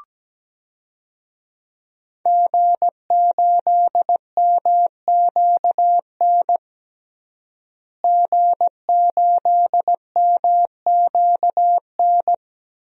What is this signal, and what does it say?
Morse code, heard as a keyed tone, about 17 wpm, 700 Hz. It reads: G8MQN G8MQN
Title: DXCC-100 Call Signs - 5 Characters - Repeated 3x - 17wpm